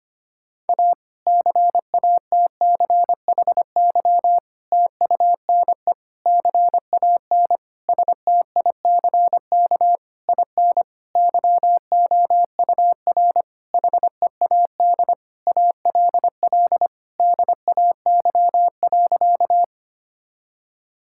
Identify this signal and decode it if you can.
Morse code, heard as a keyed tone, about 25 wpm, 700 Hz. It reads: A CATCHY TUNE CAN STICK IN YOUR HEAD ALL DAY.
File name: practice / 25wpm / Difficult Sentences with Hints - 25wpm